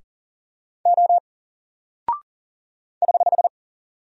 Morse code, heard as a keyed tone, about 39 wpm, 700 Hz. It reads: O <HH>
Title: Single Character (Letter:Number:Punctuation:Prosign) 40wpm